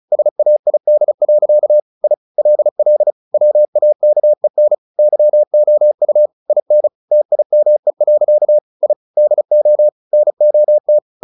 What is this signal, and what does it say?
Morse code, heard as a keyed tone, about 35 wpm, 600 Hz. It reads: SAID. I LL WAKEN YOU IN TIME. I DO NOT
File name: the old man and the sea / the old man and the sea 1 35wpm